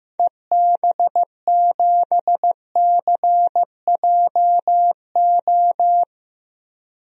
Morse code, heard as a keyed tone, about 16 wpm, 700 Hz. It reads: EB7CJO
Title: DXCC-100 Call Signs - 6 Characters - Repeated 3x - 15wpm